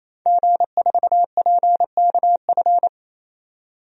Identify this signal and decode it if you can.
Morse code, heard as a keyed tone, about 28 wpm, 700 Hz. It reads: G4PKF
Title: DXCC-100 Call Signs - 5 Characters - 28wpm